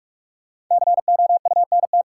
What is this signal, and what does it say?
Morse code, heard as a keyed tone, about 44 wpm, 700 Hz. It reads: COUNT
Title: Sentences from Top 1500 Words - New Word Drill - 45wpm